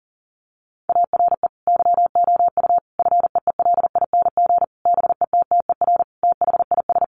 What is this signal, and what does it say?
Morse code, heard as a keyed tone, about 40 wpm, 700 Hz. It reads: ARE YOU FEELING BETTER THIS
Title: Sentences from Top 1200 Words 40wpm